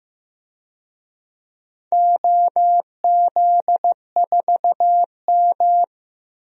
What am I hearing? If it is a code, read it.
Morse code, heard as a keyed tone, about 15 wpm, 700 Hz. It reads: OZ4M
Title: DXCC-100 Call Signs - 4 Characters - Repeated 2x - 15wpm